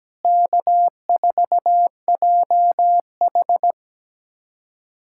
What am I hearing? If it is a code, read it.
Morse code, heard as a keyed tone, about 17 wpm, 700 Hz. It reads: K4JH
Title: USA Callsign DE Callsign - 4 Characters - 17wpm